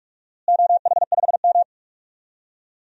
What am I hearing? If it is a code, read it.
Morse code, heard as a keyed tone, about 46 wpm, 700 Hz. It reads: OH5M
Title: DXCC-100 Call Signs - 4 Characters - Repeated 3x - 45wpm